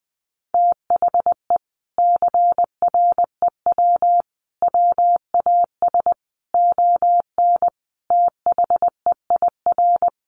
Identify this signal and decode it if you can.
Morse code, heard as a keyed tone, about 20 wpm, 700 Hz. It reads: THE CREW WAS ON THEIR
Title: Sentences from Top 1500 Words 20wpm